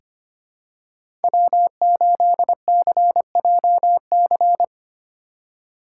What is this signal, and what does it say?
Morse code, heard as a keyed tone, about 25 wpm, 700 Hz. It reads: W8CJC
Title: USA Call Signs - 5 Characters - 2x Character Spacing - 25wpm